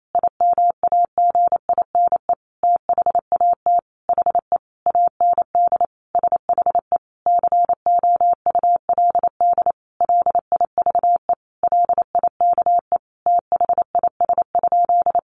Morse code, heard as a keyed tone, about 28 wpm, 700 Hz. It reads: IMAGINE THAT HE AND SHE COULD LIVE LIKE THIS?
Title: Sentences from Top 1400 Words 28wpm